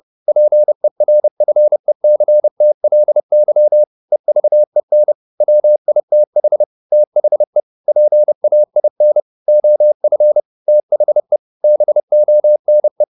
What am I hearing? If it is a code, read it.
Morse code, heard as a keyed tone, about 30 wpm, 600 Hz. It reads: PERFECTLY EVEN WITH THE PAIN OF THE BONE
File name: the old man and the sea / the old man and the sea 1 30wpm